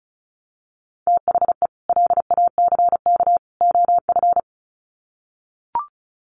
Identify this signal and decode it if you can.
Morse code, heard as a keyed tone, about 34 wpm, 700 Hz. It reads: THE LACK OF
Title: Sets of 3 Words 35wpm